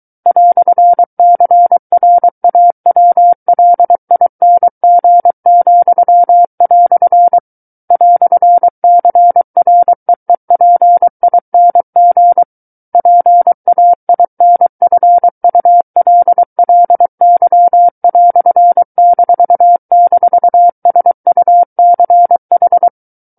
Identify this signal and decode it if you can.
Morse code, heard as a keyed tone, about 23 wpm, 700 Hz. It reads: "CRAWLING," "CREEPING PAINFULLY"--SUCH
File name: Book One / 14-Chapter 14 - 23 WPM